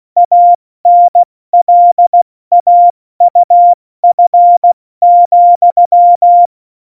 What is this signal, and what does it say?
Morse code, heard as a keyed tone, about 16 wpm, 700 Hz. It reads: ANLAUF,
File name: Morsekurs / Witze / Witz 17. November 2020, 70 F80